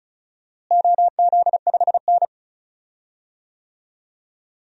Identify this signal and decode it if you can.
Morse code, heard as a keyed tone, about 34 wpm, 700 Hz. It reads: OZ5N
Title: DXCC-100 Call Signs - 4 Characters - Repeated 3x - 35wpm